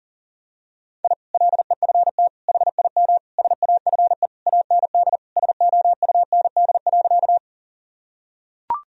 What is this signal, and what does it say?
Morse code, heard as a keyed tone, about 40 wpm, 700 Hz. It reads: I LEFT HIM SAFE AND SOUND.
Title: Sentences from Top 1000 Words - Review Part 1 - 40wpm